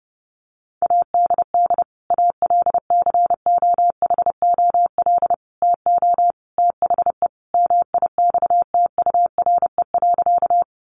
Morse code, heard as a keyed tone, about 30 wpm, 700 Hz. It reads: ADD ALCOHOL TO THE MIXTURE.
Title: Sentences from Top 1500 Words 30wpm